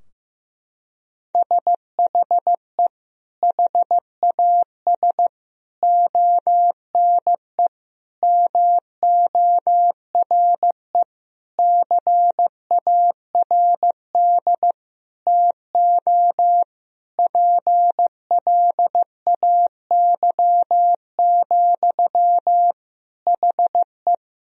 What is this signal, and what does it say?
Morse code, heard as a keyed tone, about 15 wpm, 700 Hz. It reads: SHE HAS ONE MORE CARD TO PLAY, HE
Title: Sentences from Top 1000 Words - Review Part 4 - 15wpm